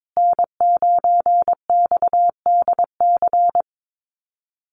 Morse code, heard as a keyed tone, about 22 wpm, 700 Hz. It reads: N9XDC
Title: USA Call Signs Repeated 2x 22wpm